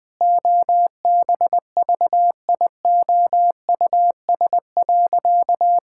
Morse code, heard as a keyed tone, about 20 wpm, 700 Hz. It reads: OBVIOUS.
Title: Sentences from Top 1600 Words 20wpm